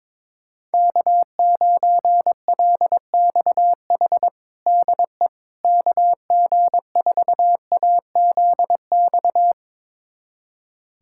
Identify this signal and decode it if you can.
Morse code, heard as a keyed tone, about 22 wpm, 700 Hz. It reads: K9LXH DE KG4AZX
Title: USA Callsign DE Callsign 22wpm